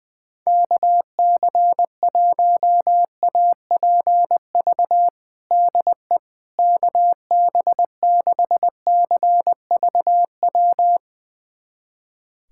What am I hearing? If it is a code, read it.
Morse code, heard as a keyed tone, about 20 wpm, 700 Hz. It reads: KC1APV DE KB6CVW
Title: USA Callsign DE Callsign 20wpm